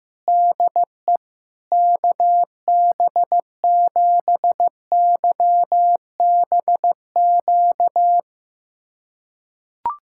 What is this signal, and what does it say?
Morse code, heard as a keyed tone, about 16 wpm, 700 Hz. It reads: DE KB7YBQ
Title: USA Callsign DE Callsign - 6 Characters - 15wpm